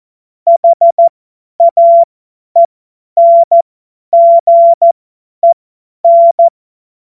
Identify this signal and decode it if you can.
Morse code, heard as a keyed tone, about 14 wpm, spaced as for 10 wpm, 680 Hz. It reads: HAENGEN
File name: Morsekurs / Witze / Witz 30. August 2022 50 F70